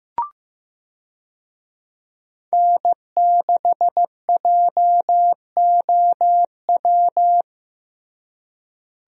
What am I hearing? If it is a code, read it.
Morse code, heard as a keyed tone, about 15 wpm, 700 Hz. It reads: N6JOW
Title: USA Call Signs - 5 Characters - Repeated 2x - 15wpm